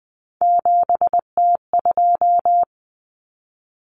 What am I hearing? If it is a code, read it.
Morse code, heard as a keyed tone, about 20 wpm, 700 Hz. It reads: 7T2
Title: Common Cut Numbers - 3-Digit 20wpm